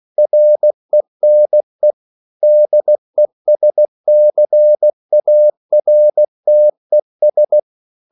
Morse code, heard as a keyed tone, about 16 wpm, 600 Hz. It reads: RENE DESCARTES
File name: CW / Practice / EnglishQuotations0116